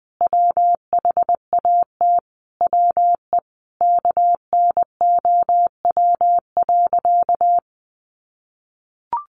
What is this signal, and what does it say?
Morse code, heard as a keyed tone, about 20 wpm, 700 Hz. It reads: WHAT WE KNOW.
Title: Sentences from Top 1200 Words 20wpm